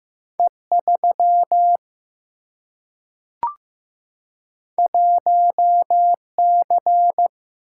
Morse code, heard as a keyed tone, about 15 wpm, 700 Hz. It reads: E 3 1 C
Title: Single-Letter Single-Number 15wpm